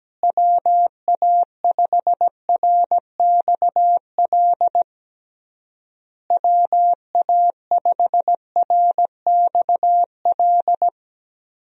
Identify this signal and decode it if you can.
Morse code, heard as a keyed tone, about 17 wpm, 700 Hz. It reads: WA5RXL WA5RXL
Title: USA Call Signs - 6 Characters - Repeated 3x - 17wpm